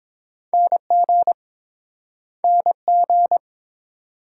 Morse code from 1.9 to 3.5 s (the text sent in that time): NG